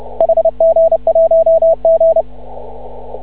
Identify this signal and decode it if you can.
Morse code, heard as a keyed tone, about 30 wpm, 660 Hz. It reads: HG1G
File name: CW / ha6oi 2